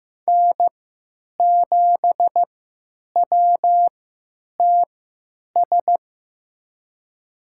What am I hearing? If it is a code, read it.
Morse code, heard as a keyed tone, about 15 wpm, spaced as for 9 wpm, 700 Hz. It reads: N7WTS
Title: USA Call Signs - 5 Characters - 3x Character Spacing - 15wpm